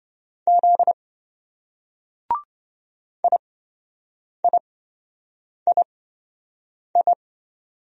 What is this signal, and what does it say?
Morse code, heard as a keyed tone, about 27 wpm, spaced as for 7 wpm, 700 Hz. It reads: Z IIII